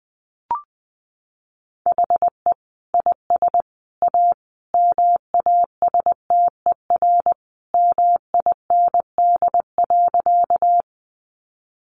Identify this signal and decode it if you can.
Morse code, heard as a keyed tone, about 20 wpm, 700 Hz. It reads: HE IS A MASTER MIND.
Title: Sentences from Top 1300 Words 20wpm